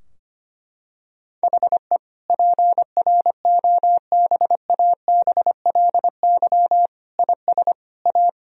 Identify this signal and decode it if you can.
Morse code, heard as a keyed tone, about 25 wpm, 700 Hz. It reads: HE PROBABLY IS A
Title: Sentences from Top 1000 Words - Review Part 4 - 25wpm